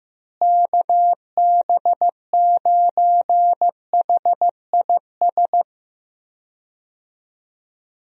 Morse code, heard as a keyed tone, about 15 wpm, 700 Hz. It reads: KB9HIS
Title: USA Call Signs - 6 Characters - Repeated 2x - 15wpm